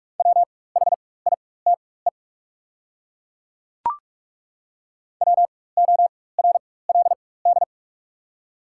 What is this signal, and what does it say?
Morse code, heard as a keyed tone, about 42 wpm, spaced as for 21 wpm, 700 Hz. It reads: WHITE WORLD
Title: 4x Character Spacing - Top 500 Words - 45wpm